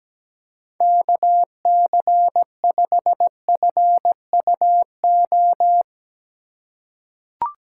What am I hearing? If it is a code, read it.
Morse code, heard as a keyed tone, about 17 wpm, 700 Hz. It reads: KC5FUO
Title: USA Call Signs - 4x Character Spacing - 17wpm